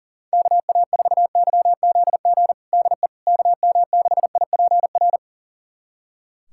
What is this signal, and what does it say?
Morse code, heard as a keyed tone, about 40 wpm, 700 Hz. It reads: KA4YZG DE KM6IPR
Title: USA Callsign DE Callsign - 6 Characters - 40wpm